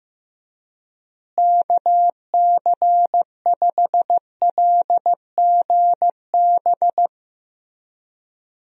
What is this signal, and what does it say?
Morse code, heard as a keyed tone, about 15 wpm, 700 Hz. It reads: KC5LGB
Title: USA Call Signs - 6 Characters - Repeated 3x - 15wpm